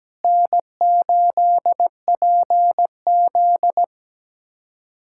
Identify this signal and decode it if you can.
Morse code, heard as a keyed tone, about 17 wpm, 700 Hz. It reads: N8PZ
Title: USA Callsign DE Callsign - 4 Characters - 17wpm